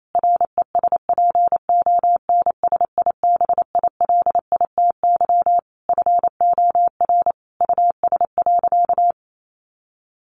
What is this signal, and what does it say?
Morse code, heard as a keyed tone, about 28 wpm, 700 Hz. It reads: RESPONSIBILITY FOR US.
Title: Sentences from Top 1500 Words 28wpm